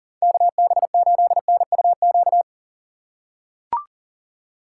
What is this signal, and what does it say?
Morse code, heard as a keyed tone, about 40 wpm, 700 Hz. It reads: KB8NUQ
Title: USA Call Signs - 6 Characters - 3x Character Spacing - 40wpm